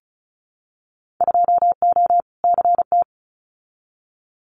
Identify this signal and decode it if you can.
Morse code, heard as a keyed tone, about 36 wpm, 690 Hz. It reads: 2O CT
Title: Winter Field Day Exchanges 35wpm